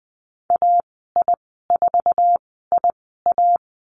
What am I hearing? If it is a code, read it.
Morse code, heard as a keyed tone, about 20 wpm, spaced as for 14 wpm, 700 Hz. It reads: AI4IA
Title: 069 Lesson 12g - Letter d - Call Signs 20wpm